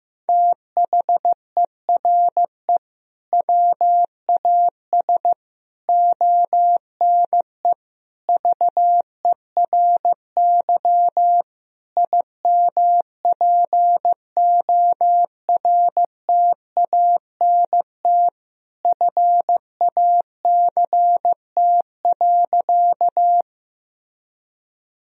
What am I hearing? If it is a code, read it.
Morse code, heard as a keyed tone, about 15 wpm, 700 Hz. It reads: THERE WAS ONE VERY IMPORTANT FACT.
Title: Sentences from Top 1000 Words - Review Part 3 - 15wpm